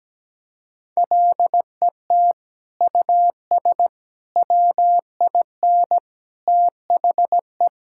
Morse code, heard as a keyed tone, about 17 wpm, 700 Hz. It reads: LET US WIN THE
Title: Sentences from Top 1600 Words 17wpm